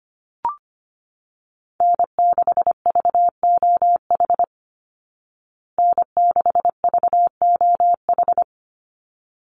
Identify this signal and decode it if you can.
Morse code, heard as a keyed tone, about 25 wpm, 700 Hz. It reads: N6VOH N6VOH
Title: Call Signs Repeated 3x - 25wpm